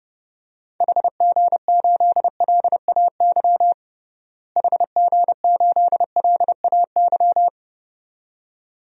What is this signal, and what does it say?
Morse code, heard as a keyed tone, about 32 wpm, 700 Hz. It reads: HG8LAY HG8LAY
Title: DXCC-100 Call Signs - 6 Characters - Repeated 2x - 30wpm